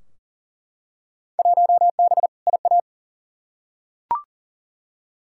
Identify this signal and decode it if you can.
Morse code, heard as a keyed tone, about 40 wpm, 700 Hz. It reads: 1B IA
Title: Field Day Exchanges Encore - 40wpm